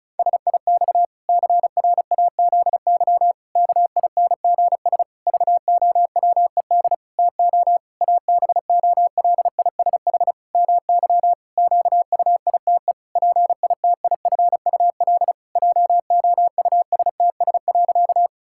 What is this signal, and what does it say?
Morse code, heard as a keyed tone, about 36 wpm, 700 Hz. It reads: SIX CRAZY KINGS VOWED TO ABOLISH MY QUITE PITIFUL JOUSTS.
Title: Pangram Challenge v2 - 35wpm